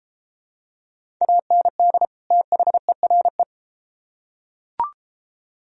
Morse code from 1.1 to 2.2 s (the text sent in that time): AND